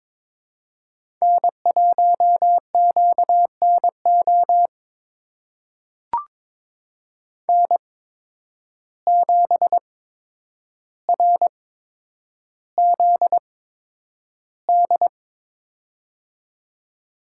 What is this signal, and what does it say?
Morse code, heard as a keyed tone, about 22 wpm, spaced as for 9 wpm, 700 Hz. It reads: N1QNO N7RZD
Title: USA Call Signs - 5 Characters - 8x Character Spacing - 22wpm